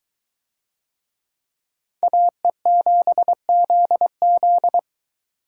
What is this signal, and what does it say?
Morse code, heard as a keyed tone, about 21 wpm, 700 Hz. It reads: AE7ZZ